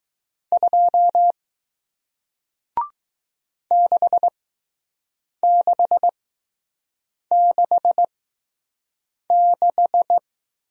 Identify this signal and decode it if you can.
Morse code, heard as a keyed tone, about 20 wpm, spaced as for 6 wpm, 700 Hz. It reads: 2 6666